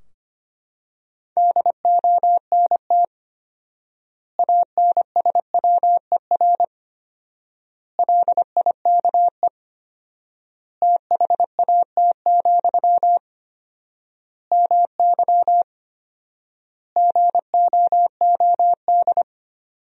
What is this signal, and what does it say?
Morse code, heard as a keyed tone, about 25 wpm, 700 Hz. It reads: DONT ANSWER LIKE THAT, MY GOOD
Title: 4x Word-Spacing - Sentences from Top 400 Words - 25wpm